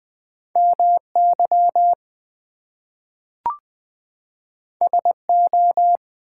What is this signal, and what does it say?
Morse code, heard as a keyed tone, about 20 wpm, 700 Hz. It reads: MY SO
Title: Two Letter Words - 20wpm